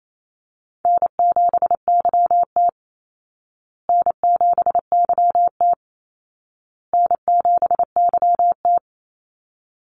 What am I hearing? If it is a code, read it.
Morse code, heard as a keyed tone, about 28 wpm, 700 Hz. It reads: N7YT N7YT N7YT
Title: Easy Call Signs Repeated 3x - 28wpm